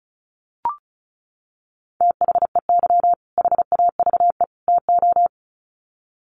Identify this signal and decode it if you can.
Morse code, heard as a keyed tone, about 34 wpm, 700 Hz. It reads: THEY HAVE TO
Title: Sets of 3 Words 35wpm